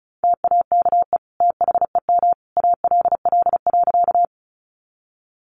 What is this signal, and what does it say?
Morse code, heard as a keyed tone, about 34 wpm, 700 Hz. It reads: TAKE THEM ALL.
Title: Sentences from Top 200 Words 35wpm